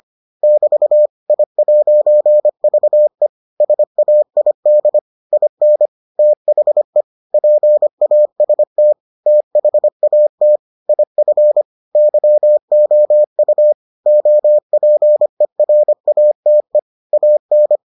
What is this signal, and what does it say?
Morse code, heard as a keyed tone, about 25 wpm, 600 Hz. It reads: <BT> I'VE SAID IN THE PAST THAT IF YOU OPERATE AN